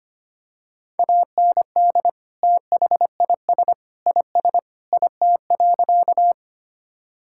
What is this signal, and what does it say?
Morse code, heard as a keyed tone, about 25 wpm, 700 Hz. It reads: AND THIS IS IT.
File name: practice / 25wpm / Sentences with 3-Word Phrases - 25wpm